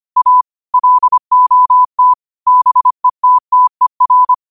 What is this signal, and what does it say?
Morse code, heard as a keyed tone, about 25 wpm, 1000 Hz. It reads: A LOT BETTER